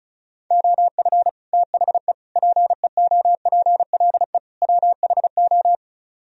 Morse code, heard as a keyed tone, about 45 wpm, 700 Hz. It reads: OF THE PEOPLE WHO